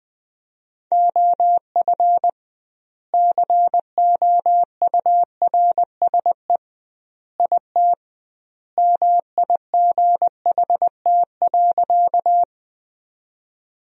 Morse code, heard as a keyed tone, about 21 wpm, 700 Hz. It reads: OF COURSE IT MIGHT.
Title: 2x Word-Spacing - Sentences from Top 500 Words - 20wpm